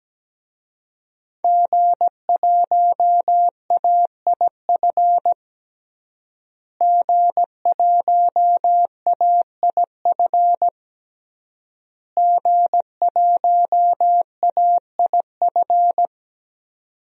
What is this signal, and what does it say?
Morse code, heard as a keyed tone, about 17 wpm, 700 Hz. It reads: G1AIF G1AIF G1AIF
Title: DXCC-100 Call Signs - 5 Characters - Repeated 3x - 17wpm